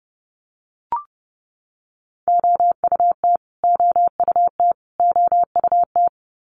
Morse code, heard as a keyed tone, about 30 wpm, 700 Hz. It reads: OUT OUT OUT
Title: Top 500 Words Repeated 30wpm